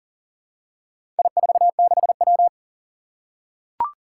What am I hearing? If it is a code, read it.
Morse code, heard as a keyed tone, about 40 wpm, 700 Hz. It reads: I46W
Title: Four Challenging Letter-Number Combinations 40wpm